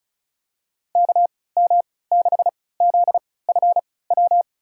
Morse code, heard as a keyed tone, about 36 wpm, spaced as for 20 wpm, 700 Hz. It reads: KM6ZFW
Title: USA Call Signs - 6 Characters - 3x Character Spacing - 35wpm